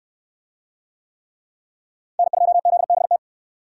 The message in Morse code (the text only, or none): N2ZBT